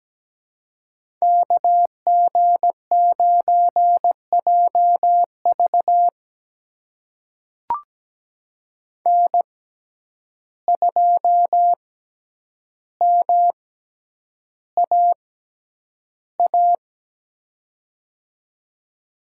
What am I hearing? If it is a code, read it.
Morse code, heard as a keyed tone, about 17 wpm, spaced as for 9 wpm, 700 Hz. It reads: KG9JV N2MAA
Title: USA Call Signs - 6x Character Spacing - 17wpm